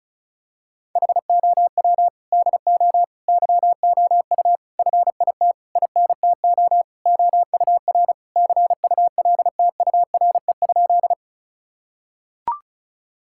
Morse code, heard as a keyed tone, about 34 wpm, 700 Hz. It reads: HOW DO YOU FIT INTO OUR CULTURE?
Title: Sentences from Top 1000 Words - Review Part 6 - 35wpm